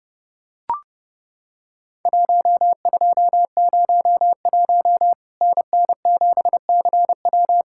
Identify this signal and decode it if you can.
Morse code, heard as a keyed tone, about 30 wpm, 700 Hz. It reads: 1201 NN7CW
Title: Worked All Europe DX Contest - QTC Exchange - 30wpm